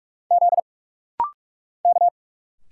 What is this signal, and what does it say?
Morse code, heard as a keyed tone, about 46 wpm, 690 Hz. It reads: Z K
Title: Single Letters - Rapid-Fire 45wpm